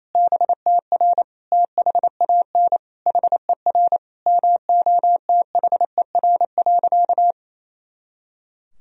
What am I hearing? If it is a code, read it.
Morse code, heard as a keyed tone, about 28 wpm, 700 Hz. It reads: BTR THAN HER MOTHER.
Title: Sentences from Top 400 Words and Abbreviations 28wpm